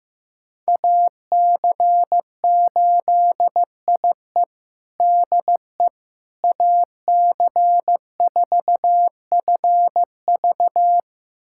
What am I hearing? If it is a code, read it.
Morse code, heard as a keyed tone, about 15 wpm, 700 Hz. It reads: AC8IE DE AC4FV
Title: USA Callsign DE Callsign - 5 Characters - 15wpm